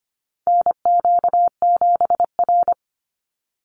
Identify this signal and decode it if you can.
Morse code, heard as a keyed tone, about 25 wpm, 700 Hz. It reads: NQ7R
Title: NA SOTA Chaser Callsigns 25wpm